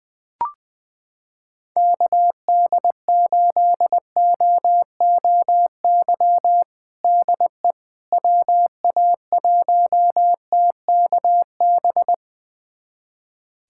Morse code, heard as a keyed tone, about 20 wpm, 700 Hz. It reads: KD8OOY DE WA1TKB
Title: USA Callsign DE Callsign - 6 Characters - 20wpm